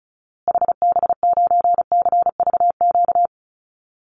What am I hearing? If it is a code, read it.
Morse code, heard as a keyed tone, about 34 wpm, 700 Hz. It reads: HB9CVQ
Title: Call Signs Repeated 2x - 35wpm